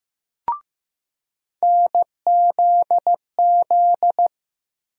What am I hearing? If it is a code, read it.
Morse code, heard as a keyed tone, about 15 wpm, 700 Hz. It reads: NZZ
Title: USA Call Sign Postfixes - 3 Letters 15wpm